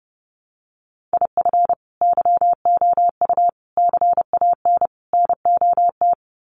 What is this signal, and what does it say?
Morse code, heard as a keyed tone, about 30 wpm, 700 Hz. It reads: IF YOU CAN NOT
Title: Sets of 4 Words 30wpm